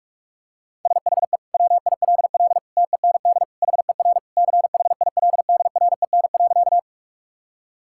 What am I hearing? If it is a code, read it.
Morse code, heard as a keyed tone, about 46 wpm, 700 Hz. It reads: SHE WILL TEND HER CHILDREN.
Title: Sentences from Top 1000 Words - Review Part 5 - 45wpm